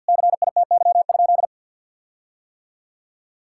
Morse code, heard as a keyed tone, about 50 wpm, 700 Hz. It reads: CITY?
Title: Sentences from Top 1600 Words 50wpm